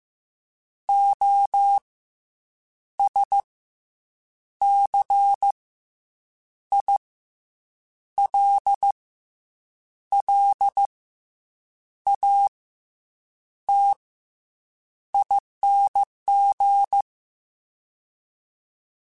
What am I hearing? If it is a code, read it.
Morse code, heard as a keyed tone, about 15 wpm, 800 Hz. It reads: O S C I L L A T ING